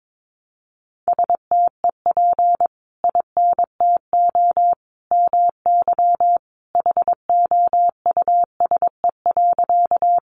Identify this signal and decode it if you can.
Morse code, heard as a keyed tone, about 22 wpm, 700 Hz. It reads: STEP INTO MY HOUSE.
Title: Sentences from Top 600 Words 22wpm